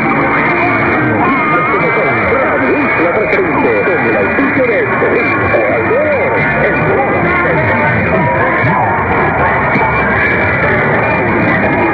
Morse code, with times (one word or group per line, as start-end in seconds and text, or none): none